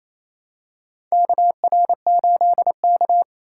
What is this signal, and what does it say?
Morse code, heard as a keyed tone, about 27 wpm, 700 Hz. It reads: KR8K
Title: USA Call Signs - 4 Characters - 4x Character Spacing - 28wpm